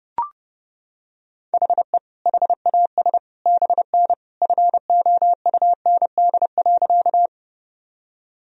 Morse code, heard as a keyed tone, about 30 wpm, 700 Hz. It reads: HE HAS BN FOUND.
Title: Sentences from Top 300 Words and Abbreviations 30wpm